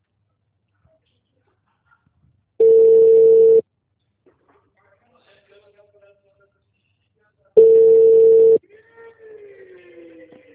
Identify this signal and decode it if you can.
Ringback tone (China, 450 Hz), with speech.